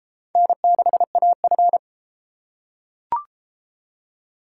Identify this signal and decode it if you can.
Morse code, heard as a keyed tone, about 32 wpm, 700 Hz. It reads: N6AF